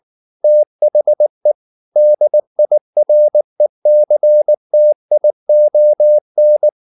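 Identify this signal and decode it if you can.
Morse code, heard as a keyed tone, about 19 wpm, 600 Hz. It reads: THE DIRECTION
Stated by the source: Morse code at 19 wpm from CW med Gnister, reading Ken Follett's novel Never in English.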